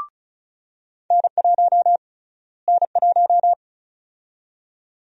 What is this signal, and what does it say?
Morse code, heard as a keyed tone, about 36 wpm, 700 Hz. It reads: N1 N1
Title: USA Call Sign Prefixes - Repeated 2x - 35wpm